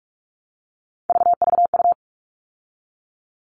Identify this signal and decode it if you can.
Morse code, heard as a keyed tone, about 44 wpm, 700 Hz. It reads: VVU
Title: All Cut Numbers - 3-Digit 45wpm